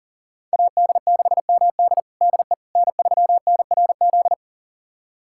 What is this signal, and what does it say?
Morse code, heard as a keyed tone, about 40 wpm, 700 Hz. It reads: AD6MD DE N3NRZ
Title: USA Callsign DE Callsign - 5 Characters - 40wpm